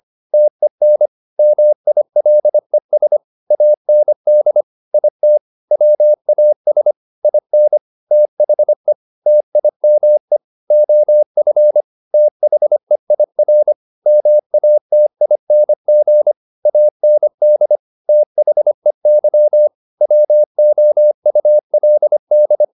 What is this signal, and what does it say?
Morse code, heard as a keyed tone, about 25 wpm, 600 Hz. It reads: TEN MILES AND IT WAS IN THE TIME OF THEIR MATING AND THEY WOULD